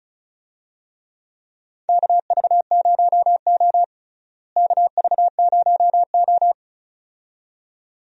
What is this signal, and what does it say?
Morse code, heard as a keyed tone, about 36 wpm, 700 Hz. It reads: KV0O KV0O
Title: USA Call Signs - 4 Characters - Repeated 2x - 35wpm